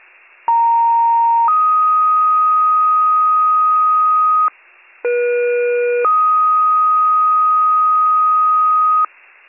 Two-tone page, heard as a one-second tone, then a three-second tone, about 930 then 1250 Hz, about 480 then 1250 Hz. It is heard from 0.5 to 9.1 s.